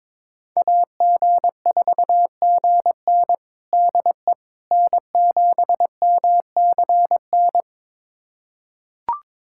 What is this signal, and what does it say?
Morse code, heard as a keyed tone, about 21 wpm, 700 Hz. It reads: AG4GN DE N7MCN
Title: USA Callsign DE Callsign - 5 Characters - 22wpm